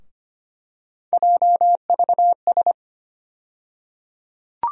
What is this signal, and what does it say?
Morse code, heard as a keyed tone, about 25 wpm, 700 Hz. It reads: JVS
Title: USA Call Sign Postfixes - 3 Letters 25wpm